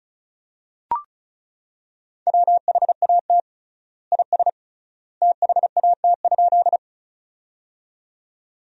Morse code, heard as a keyed tone, about 36 wpm, 700 Hz. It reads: WHAT IS THAT?
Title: 3x Word-Spacing - Sentences from Top 100 Words - 35wpm